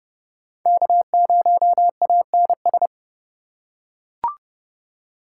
Morse code, heard as a keyed tone, about 31 wpm, 700 Hz. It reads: K0ANS
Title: USA Call Signs - 5 Characters - 3x Character Spacing - 30wpm